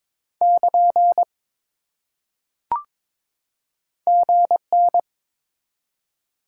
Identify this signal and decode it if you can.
Morse code, heard as a keyed tone, about 22 wpm, 700 Hz. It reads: <KN> GN
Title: Top 90 QSO Elements 22wpm